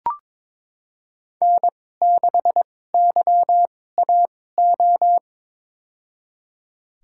Morse code, heard as a keyed tone, about 22 wpm, spaced as for 16 wpm, 700 Hz. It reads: N6YAO
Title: USA Call Signs - 5 Characters - 2x Character Spacing - 22wpm